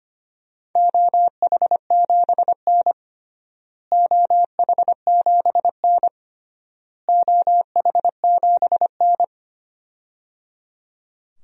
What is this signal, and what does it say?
Morse code, heard as a keyed tone, about 25 wpm, 700 Hz. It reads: OH7N OH7N OH7N
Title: DXCC-100 Call Signs - 4 Characters - Repeated 3x - 25wpm